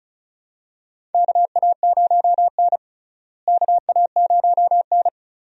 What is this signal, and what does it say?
Morse code, heard as a keyed tone, about 36 wpm, 700 Hz. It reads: KA0N KA0N
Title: USA Call Signs - 4 Characters - Repeated 3x - 35wpm